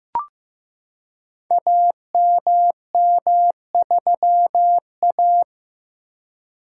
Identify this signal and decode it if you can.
Morse code, heard as a keyed tone, about 15 wpm, 700 Hz. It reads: AMM3A
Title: Code Groups - 5 Letter:Numbers - 15wpm